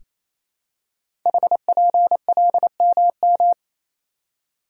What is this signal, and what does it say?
Morse code, heard as a keyed tone, about 27 wpm, 700 Hz. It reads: HPLMM
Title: Code Groups - 5 Letters - 28wpm